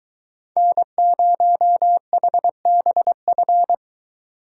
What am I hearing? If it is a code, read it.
Morse code, heard as a keyed tone, about 23 wpm, 700 Hz. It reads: N0HBF